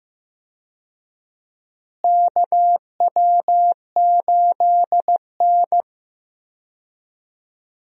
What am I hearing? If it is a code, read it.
Morse code, heard as a keyed tone, about 15 wpm, 700 Hz. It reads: KW8N
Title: USA Call Signs - 4 Characters - Repeated 2x - 15wpm